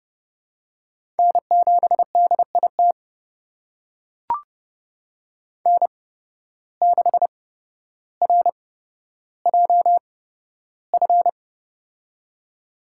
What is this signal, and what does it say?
Morse code, heard as a keyed tone, about 30 wpm, spaced as for 13 wpm, 700 Hz. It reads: N7DIT N6RJF
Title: USA Call Signs - 5 Characters - 8x Character Spacing - 30wpm